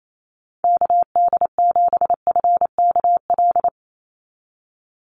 Morse code, heard as a keyed tone, about 28 wpm, 700 Hz. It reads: KD7FKL
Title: USA Call Signs 28wpm